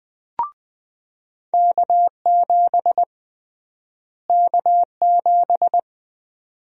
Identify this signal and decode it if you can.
Morse code, heard as a keyed tone, about 20 wpm, 700 Hz. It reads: K7 K7
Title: USA Call Sign Prefixes - Repeated 2x - 20wpm